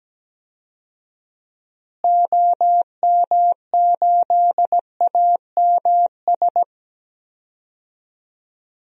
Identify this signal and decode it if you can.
Morse code, heard as a keyed tone, about 17 wpm, 700 Hz. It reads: OM8AMS
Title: DXCC-100 Call Signs - 6 Characters - Repeated 3x - 17wpm